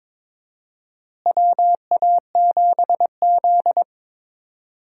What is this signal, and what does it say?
Morse code, heard as a keyed tone, about 21 wpm, 700 Hz. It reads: WA7Z
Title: USA Call Signs - 4 Characters - Repeated 2x - 22wpm